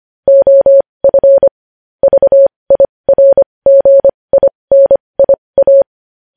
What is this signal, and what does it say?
Morse code, heard as a keyed tone, about 25 wpm, spaced as for 20 wpm, 550 Hz. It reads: OF VIRGINIA
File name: podcast / News headlines-1x-25x20